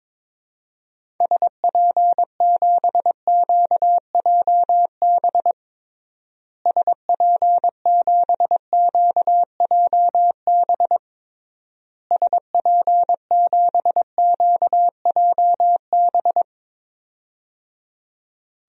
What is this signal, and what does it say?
Morse code, heard as a keyed tone, about 22 wpm, 700 Hz. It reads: SP7QJB SP7QJB SP7QJB
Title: DXCC-100 Call Signs - 6 Characters - Repeated 3x - 22wpm